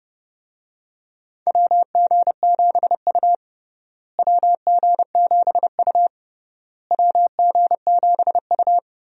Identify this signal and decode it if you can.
Morse code, heard as a keyed tone, about 32 wpm, 700 Hz. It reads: WG7U WG7U WG7U
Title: USA Call Signs - 4 Characters - Repeated 3x - 30wpm